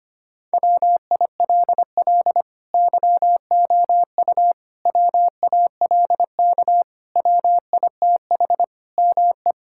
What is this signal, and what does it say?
Morse code, heard as a keyed tone, about 25 wpm, 700 Hz. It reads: WILL YOU WALK WITH ME
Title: Sentences from Top 1000 Words - Review Part 6 - 25wpm